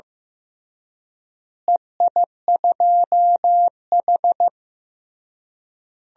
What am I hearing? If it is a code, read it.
Morse code, heard as a keyed tone, about 15 wpm, 700 Hz. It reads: EI2H
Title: DXCC-100 Call Signs - 4 Characters - Repeated 3x - 15wpm